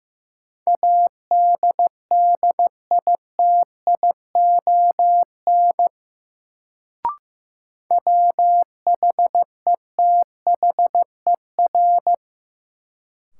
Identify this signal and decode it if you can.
Morse code, heard as a keyed tone, about 15 wpm, 700 Hz. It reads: ADDITION WHETHER
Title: Sets of 3 Words - New Word Drill - 15wpm